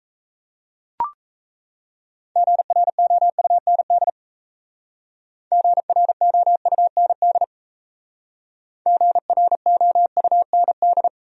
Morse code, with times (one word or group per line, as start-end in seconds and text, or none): 2.4-4.1 GROUND
5.5-7.4 GROUND
8.9-11.1 GROUND